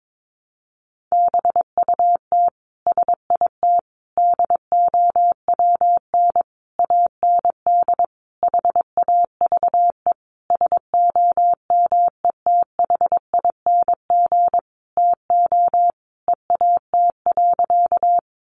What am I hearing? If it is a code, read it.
Morse code, heard as a keyed tone, about 22 wpm, 700 Hz. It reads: BUT SIT DOWN AND HAVE SOMETHING TO EAT.
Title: Sentences from Top 600 Words 22wpm